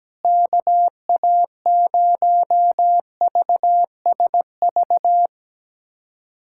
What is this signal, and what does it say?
Morse code, heard as a keyed tone, about 17 wpm, 700 Hz. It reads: KA0VSV
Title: USA Callsign DE Callsign 17wpm